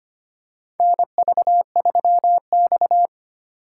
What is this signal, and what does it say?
Morse code, heard as a keyed tone, about 25 wpm, 700 Hz. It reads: NV3X
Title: USA Call Signs - 4 Characters - 4x Character Spacing - 25wpm